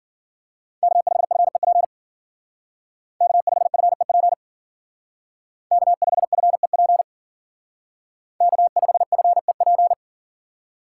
Morse code, heard as a keyed tone, about 52 wpm, 700 Hz. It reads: K5FEP K5FEP K5FEP K5FEP